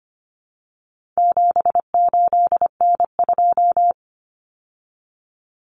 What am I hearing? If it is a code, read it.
Morse code, heard as a keyed tone, about 25 wpm, 700 Hz. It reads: 78N2
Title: Common Cut Numbers - 4-Digit 25wpm